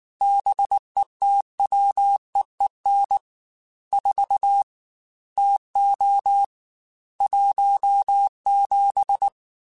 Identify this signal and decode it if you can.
Morse code, heard as a keyed tone, about 19 wpm, 800 Hz. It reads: BETWEEN 4 TO 17
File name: Weather Sets / Weather Sets-mp3s / Weather 4 21 @18